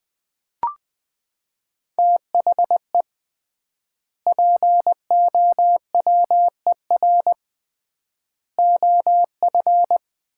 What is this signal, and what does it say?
Morse code, heard as a keyed tone, about 20 wpm, 700 Hz. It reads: THE POWER OF
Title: 3x Word-Spacing - Sets of 4 Words - 20wpm